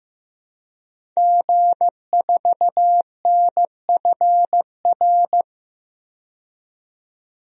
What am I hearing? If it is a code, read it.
Morse code, heard as a keyed tone, about 15 wpm, 700 Hz. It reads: G4NFR
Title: DXCC-100 Call Signs - 5 Characters - 15wpm